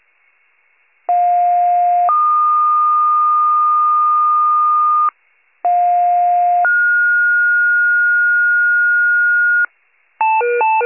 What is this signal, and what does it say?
Two-tone page, heard as a one-second tone, then a three-second tone, about 710 then 1220 Hz, about 710 then 1430 Hz, with noise. It is heard from 1.1 to 9.7 s.
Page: 82 FIRE